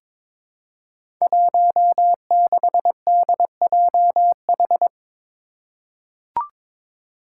Morse code, heard as a keyed tone, about 22 wpm, 700 Hz. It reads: 16DJH
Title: Five Challenging Letter-Number Combinations 22wpm